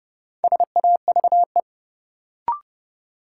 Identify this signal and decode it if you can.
Morse code, heard as a keyed tone, about 30 wpm, 700 Hz. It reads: SAVE
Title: Sentences from Top 1100 Words - New Word Drill Alt - 30wpm